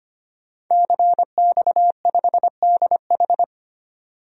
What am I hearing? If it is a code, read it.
Morse code, heard as a keyed tone, about 25 wpm, 700 Hz. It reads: CX5DH
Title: DXCC-100 Call Signs - 5 Characters - 25wpm